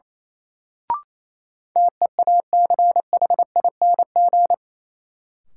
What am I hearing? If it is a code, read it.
Morse code, heard as a keyed tone, about 28 wpm, 700 Hz. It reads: TEACHING
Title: Sentences from Top 1100 Words - New Word Drill - 28wpm